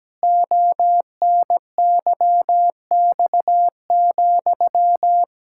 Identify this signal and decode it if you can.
Morse code, heard as a keyed tone, about 17 wpm, 700 Hz. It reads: ONYX,
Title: Pangram Challenge v2 - 17wpm